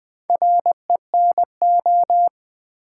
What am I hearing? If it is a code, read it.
Morse code, heard as a keyed tone, about 20 wpm, 700 Hz. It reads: RENO
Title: Build A Word - 2 Letters At A Time - 8-Letter Words - 20wpm